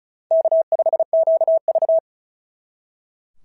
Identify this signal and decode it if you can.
Morse code, heard as a keyed tone, about 35 wpm, 660 Hz. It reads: K5QV
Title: SOTA QSO Practice 35wpm